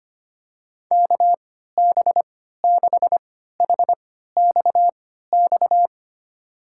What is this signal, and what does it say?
Morse code, heard as a keyed tone, about 25 wpm, spaced as for 14 wpm, 700 Hz. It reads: KB6HXX
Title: USA Call Signs - 6 Characters - 3x Character Spacing - 25wpm